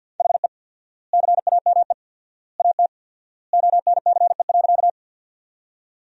Morse code, heard as a keyed tone, about 50 wpm, 700 Hz. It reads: HE CAME AT ONCE.
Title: 4x Word-Spacing - Sentences from Top 300 Words - 50wpm